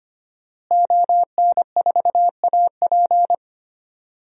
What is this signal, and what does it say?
Morse code, heard as a keyed tone, about 25 wpm, 700 Hz. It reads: ON4AP
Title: DXCC-100 Call Signs - 5 Characters - Repeated 2x - 25wpm